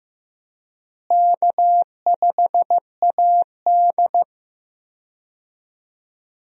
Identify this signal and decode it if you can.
Morse code, heard as a keyed tone, about 15 wpm, 700 Hz. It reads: K5AD
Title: USA Call Signs - 4 Characters - Repeated 2x - 15wpm